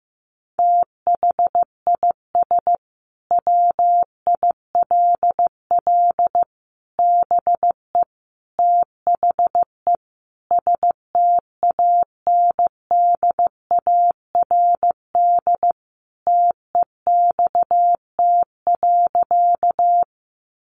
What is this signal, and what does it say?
Morse code, heard as a keyed tone, about 15 wpm, 700 Hz. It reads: THIS WILL BE THE STANDARD TEXT.
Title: Sentences from Top 1200 Words 15wpm